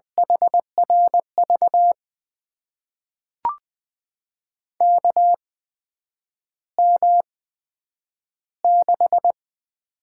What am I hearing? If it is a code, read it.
Morse code, heard as a keyed tone, about 20 wpm, spaced as for 9 wpm, 700 Hz. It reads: HRV KM6
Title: USA Call Signs - 6 Characters - 8x Character Spacing - 20wpm